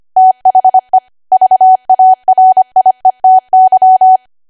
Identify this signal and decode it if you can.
Morse code, heard as a keyed tone, about 25 wpm, 750 Hz. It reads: THE VARIETY
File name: cwa / W1AW-Level3 / 25WPM